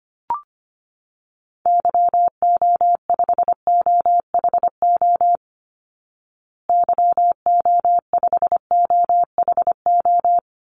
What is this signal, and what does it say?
Morse code, heard as a keyed tone, about 25 wpm, 700 Hz. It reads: YO5OHO YO5OHO
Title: Call Signs Repeated 3x - 25wpm